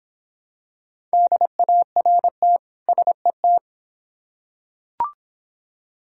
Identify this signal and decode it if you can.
Morse code, heard as a keyed tone, about 26 wpm, 700 Hz. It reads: DART SET